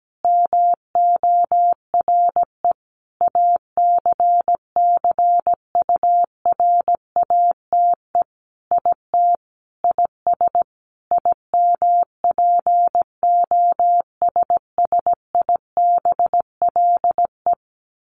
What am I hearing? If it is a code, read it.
Morse code, heard as a keyed tone, about 17 wpm, 700 Hz. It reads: MORE ACCURATE IT IS IMPOSSIBLE
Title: Sentences from Top 1200 Words 17wpm